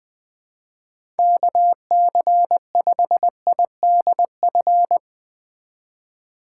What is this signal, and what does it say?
Morse code, heard as a keyed tone, about 20 wpm, 700 Hz. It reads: KC5IDF
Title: USA Call Signs - 6 Characters - 3x Character Spacing - 20wpm